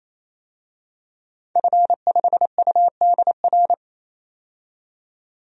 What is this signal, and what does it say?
Morse code, heard as a keyed tone, about 27 wpm, 700 Hz. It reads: F5UDR
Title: DXCC-100 Call Signs - 5 Characters - 28wpm